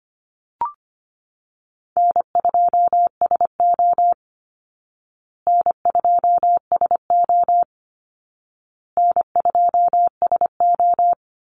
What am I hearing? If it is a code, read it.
Morse code, heard as a keyed tone, about 25 wpm, 700 Hz. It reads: N2SO N2SO N2SO
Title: Easy Call Signs Repeated 3x - 25wpm